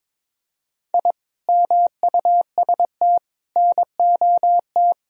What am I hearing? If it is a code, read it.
Morse code, heard as a keyed tone, about 22 wpm, 700 Hz. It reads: I MUST NOT
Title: Sentences from Top 1000 Words - Review Part 2 - 22wpm